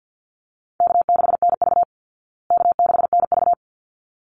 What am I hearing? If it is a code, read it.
Morse code, heard as a keyed tone, about 50 wpm, 700 Hz. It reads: K6NV K6NV
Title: Easy Call Signs Repeated 2x - 50wpm